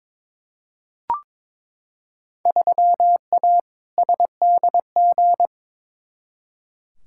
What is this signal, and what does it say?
Morse code, heard as a keyed tone, about 22 wpm, 700 Hz. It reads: 3A SDG
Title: Field Day Exchanges - 22wpm